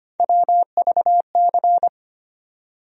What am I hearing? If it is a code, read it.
Morse code, heard as a keyed tone, about 25 wpm, 700 Hz. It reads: WVC
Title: USA Call Sign Postfixes - 3 Letters 25wpm